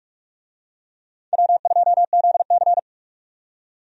Morse code, heard as a keyed tone, about 46 wpm, 700 Hz. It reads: W2ZC
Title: USA Call Signs - 4 Characters - 45wpm